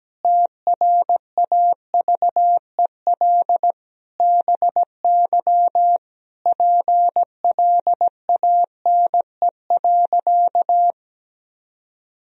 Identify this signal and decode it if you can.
Morse code, heard as a keyed tone, about 17 wpm, 700 Hz. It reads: TRAVEL BY PLANE.
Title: Sentences from Top 1000 Words - Review Part 3 - 17wpm